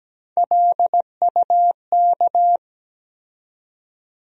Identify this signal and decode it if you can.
Morse code, heard as a keyed tone, about 17 wpm, 700 Hz. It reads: LUK
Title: USA Call Sign Postfixes - 3 Letters 17wpm